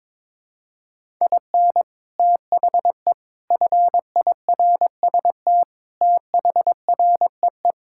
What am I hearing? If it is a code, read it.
Morse code, heard as a keyed tone, about 22 wpm, 700 Hz. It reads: IN THE FIRST THREE
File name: practice / 22wpm / Sets of 8 Words - 22wpm